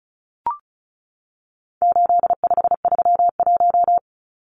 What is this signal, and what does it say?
Morse code, heard as a keyed tone, about 34 wpm, 700 Hz. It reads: 8531
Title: Numbers - 4-Digit 35wpm